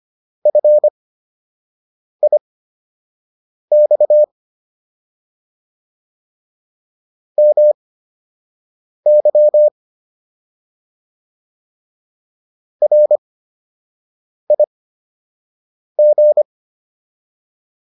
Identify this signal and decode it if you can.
Morse code, heard as a keyed tone, about 25 wpm, spaced as for 6 wpm, 600 Hz. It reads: FIX MY RIG